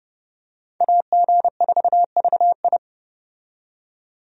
Morse code, heard as a keyed tone, about 30 wpm, 700 Hz. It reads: AG4VI
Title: USA Call Signs - 5 Characters - 30wpm